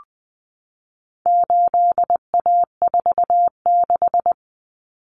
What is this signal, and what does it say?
Morse code, heard as a keyed tone, about 20 wpm, 700 Hz. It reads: 8A46
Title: Common Cut Numbers - 4-Digit 20wpm